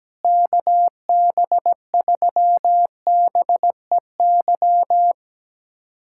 Morse code, heard as a keyed tone, about 17 wpm, 700 Hz. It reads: KB3BEY
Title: USA Callsign DE Callsign - 6 Characters - 17wpm